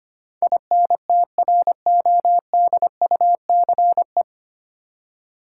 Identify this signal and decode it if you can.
Morse code, heard as a keyed tone, about 25 wpm, 700 Hz. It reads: INTRODUCE
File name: practice / 25wpm / 9 Letter Words - First 3 Letters Given - 25wpm